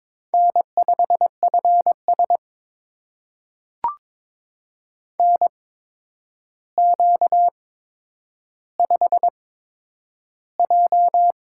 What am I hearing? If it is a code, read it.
Morse code, heard as a keyed tone, about 22 wpm, spaced as for 9 wpm, 700 Hz. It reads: N5FS NQ5J
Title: USA Call Signs - 4 Characters - 8x Character Spacing - 22wpm